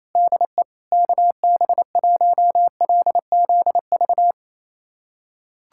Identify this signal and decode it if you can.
Morse code, heard as a keyed tone, about 28 wpm, 700 Hz. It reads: DE KB1LZV
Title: USA Callsign DE Callsign - 6 Characters - 28wpm